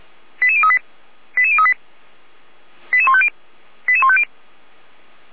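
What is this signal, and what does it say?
Five-tone selective call (ZVEI), heard as quick runs of five tones, tones of about 70 ms. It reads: 88038 88038 88150 88150
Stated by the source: ZVEI five-tone selective call received by SDR.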